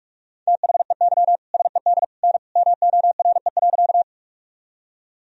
Morse code, heard as a keyed tone, about 46 wpm, 700 Hz. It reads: THEY SED N MORE.
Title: Sentences from Top 100 Words and Abbreviations 45wpm